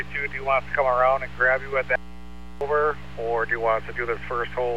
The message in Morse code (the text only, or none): none